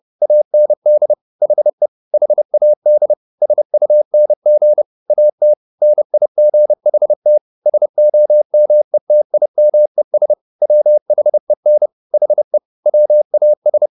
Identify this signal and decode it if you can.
Morse code, heard as a keyed tone, about 30 wpm, 600 Hz. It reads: AND HE HAD SUNG AT NIGHT SOMETIMES WHEN HE WAS
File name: the old man and the sea / the old man and the sea 1 30wpm